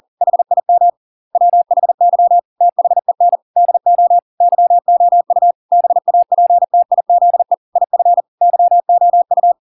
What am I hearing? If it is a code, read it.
Morse code, heard as a keyed tone, about 40 wpm, 700 Hz. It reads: HIM WHY THEN DO YOU BAPTIZE IF YOU